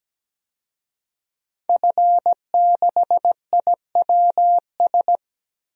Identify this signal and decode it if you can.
Morse code, heard as a keyed tone, about 17 wpm, 700 Hz. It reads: F6IWS
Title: DXCC-100 Call Signs - 5 Characters - 17wpm